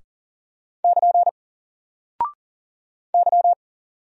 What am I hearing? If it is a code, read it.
Morse code, heard as a keyed tone, about 39 wpm, 700 Hz. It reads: <KN> Y
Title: Single Character (Letter:Number:Punctuation:Prosign) 40wpm